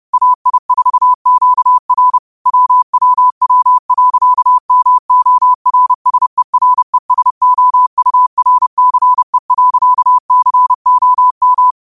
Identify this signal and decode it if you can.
Morse code, heard as a keyed tone, about 30 wpm, 1000 Hz. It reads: AI4QR WWW.MORSERESOURCE.COM
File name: qotdfiles / 30wpm